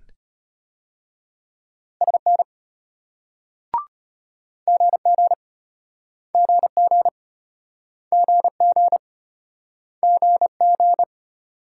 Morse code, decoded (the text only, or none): SN GG GG GG GG